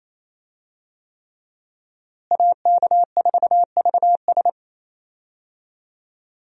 Morse code, heard as a keyed tone, about 28 wpm, 700 Hz. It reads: AK4VS
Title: USA Call Signs - 5 Characters - Repeated 3x - 28wpm